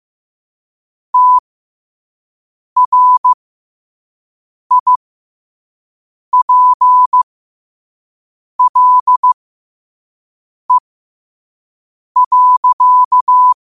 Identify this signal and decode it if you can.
Morse code, heard as a keyed tone, about 15 wpm, spaced as for 5 wpm, 1000 Hz. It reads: TRIPLE.
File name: qotdfiles / 5wpm